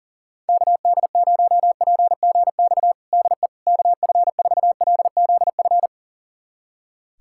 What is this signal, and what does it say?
Morse code, heard as a keyed tone, about 40 wpm, 700 Hz. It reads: KD0PGX DE KF4LZF
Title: USA Callsign DE Callsign 40wpm